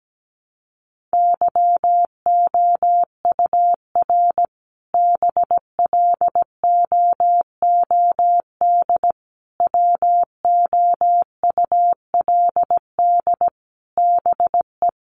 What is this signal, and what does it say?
Morse code, heard as a keyed tone, about 17 wpm, 700 Hz. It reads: YOUR BLOOD WOULD BE
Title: Sentences from Top 1100 Words 17wpm